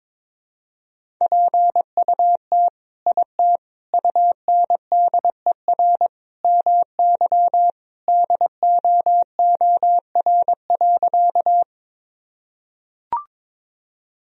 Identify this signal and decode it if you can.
Morse code, heard as a keyed tone, about 21 wpm, 700 Hz. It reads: PUT IT UNDER MY DOOR.
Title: Sentences from Top 1000 Words - Review Part 1 - 22wpm